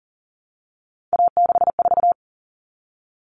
Morse code, heard as a keyed tone, about 40 wpm, 700 Hz. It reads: A64
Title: All Cut Numbers - 3-Digit 40wpm